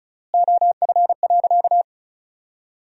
Morse code, heard as a keyed tone, about 34 wpm, 700 Hz. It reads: OF.
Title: 4x Word-Spacing - Sentences from Top 300 Words - 35wpm